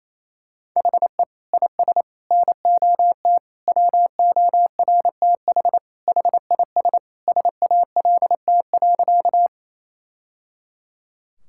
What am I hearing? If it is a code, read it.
Morse code, heard as a keyed tone, about 28 wpm, 700 Hz. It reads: HE IS NOT WORTH HIS SALT.
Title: Sentences from Top 1000 Words - Review Part 5 - 28wpm